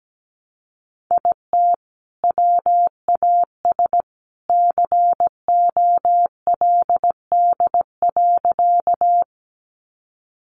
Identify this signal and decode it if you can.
Morse code, heard as a keyed tone, about 17 wpm, 700 Hz. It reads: IT WAS COLD.
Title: Sentences from Top 400 Words 17wpm